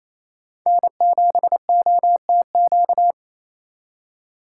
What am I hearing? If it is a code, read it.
Morse code, heard as a keyed tone, about 35 wpm, 700 Hz. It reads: N7OTQ